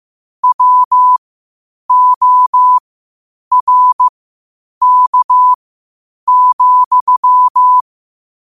Morse code, heard as a keyed tone, about 15 wpm, spaced as for 8 wpm, 1000 Hz. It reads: WORK,
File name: qotdfiles / 8wpm